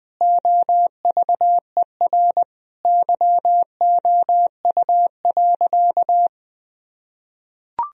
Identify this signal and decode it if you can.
Morse code, heard as a keyed tone, about 20 wpm, 700 Hz. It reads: OVER YOU.
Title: Sentences from Top 1000 Words - Review Part 1 - 20wpm